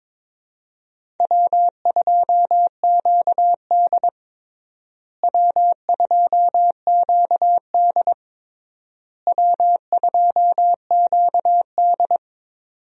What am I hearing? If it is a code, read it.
Morse code, heard as a keyed tone, about 22 wpm, 700 Hz. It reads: W2QD W2QD W2QD
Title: USA Call Signs - 4 Characters - Repeated 3x - 22wpm